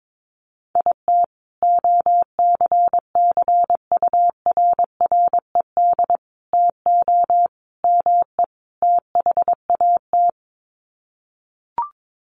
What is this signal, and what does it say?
Morse code, heard as a keyed tone, about 22 wpm, 700 Hz. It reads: IT OCCURRED TO ME THAT
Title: Sets of 5 Words Encore 22wpm